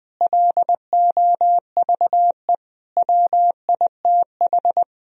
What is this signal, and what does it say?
Morse code, heard as a keyed tone, about 20 wpm, 700 Hz. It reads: LOVE WITH
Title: Sentences from Top 1000 Words - Review Part 4 - 20wpm